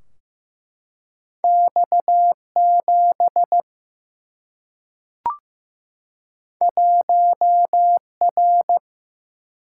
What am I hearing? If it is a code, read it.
Morse code, heard as a keyed tone, about 15 wpm, 700 Hz. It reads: X7 1R
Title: Single-Letter Single-Number 15wpm